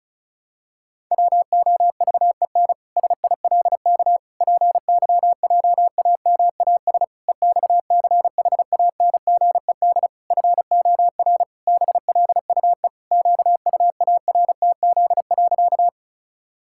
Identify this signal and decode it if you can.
Morse code, heard as a keyed tone, about 34 wpm, 700 Hz. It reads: WOVEN SILK PYJAMAS EXCHANGED FOR BLUE QUARTZ.
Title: Pangram Challenge v2 - 35wpm